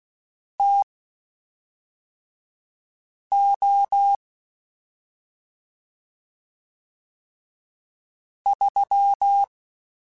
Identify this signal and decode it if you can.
Morse code, heard as a keyed tone, about 16 wpm, spaced as for 3 wpm, 800 Hz. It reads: TO 3